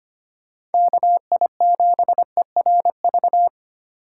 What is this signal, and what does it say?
Morse code, heard as a keyed tone, about 33 wpm, 700 Hz. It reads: KI7ERV